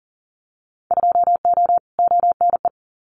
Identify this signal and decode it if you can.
Morse code, heard as a keyed tone, about 39 wpm, 700 Hz. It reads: 2O ONE
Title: Winter Field Day Exchanges Encore 40wpm